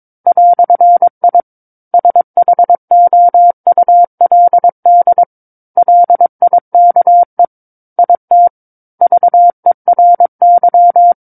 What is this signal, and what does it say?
Morse code, heard as a keyed tone, about 22 wpm, 700 Hz. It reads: "I SHOULD LIKE IT VERY
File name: 06-Chapter-06 22 WPM